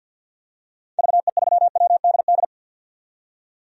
Morse code, heard as a keyed tone, about 53 wpm, 700 Hz. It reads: VE3WDD
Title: DXCC-100 Call Signs - 6 Characters - 50wpm